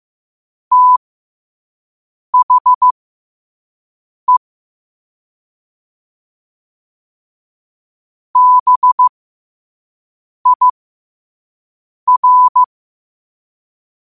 Morse code, heard as a keyed tone, about 15 wpm, spaced as for 5 wpm, 1000 Hz. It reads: THE BIR
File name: qotdfiles / 5wpm